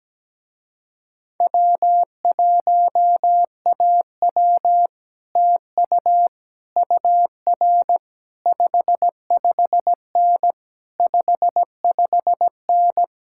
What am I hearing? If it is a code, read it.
Morse code, heard as a keyed tone, about 17 wpm, 700 Hz. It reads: W1AW TU UR 55N 55N
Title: Speed-Racing Common QSO Phrases Encore - 17wpm